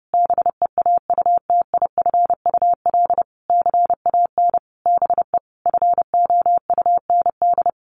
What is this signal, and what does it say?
Morse code, heard as a keyed tone, about 30 wpm, 700 Hz. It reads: BEAUTIFUL CAN BE FOUND
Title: Sentences from Top 1500 Words 30wpm